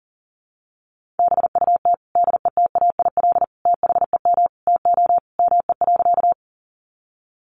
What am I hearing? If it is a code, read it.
Morse code, heard as a keyed tone, about 40 wpm, 700 Hz. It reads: BUT DETAIL THEM TO ME.
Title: Sentences from Top 1500 Words 40wpm